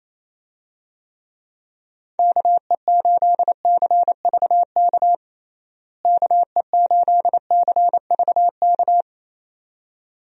Morse code, heard as a keyed tone, about 28 wpm, 700 Hz. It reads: KE8CVK KE8CVK
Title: USA Call Signs - 6 Characters - Repeated 2x - 28wpm